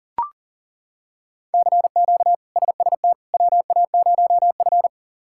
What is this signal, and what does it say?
Morse code, heard as a keyed tone, about 40 wpm, 690 Hz. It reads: CQ SST WA0F
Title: K1USN SST QSO Practice - 40wpm